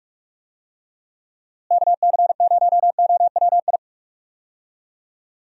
Morse code, heard as a keyed tone, about 41 wpm, 700 Hz. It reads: KC0OWI
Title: USA Call Signs - 6 Characters - Repeated 3x - 45wpm